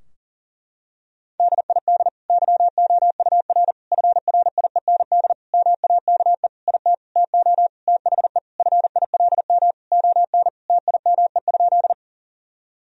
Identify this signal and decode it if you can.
Morse code, heard as a keyed tone, about 40 wpm, 700 Hz. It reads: DID YOUR FRIEND MAKE IT TO THE FILM ON TIME?
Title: Sentences from Top 1000 Words - Review Part 3 - 40wpm